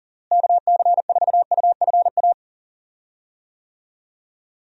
Morse code, heard as a keyed tone, about 39 wpm, 700 Hz. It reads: KC4UFA
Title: USA Call Signs - 6 Characters - Repeated 2x - 40wpm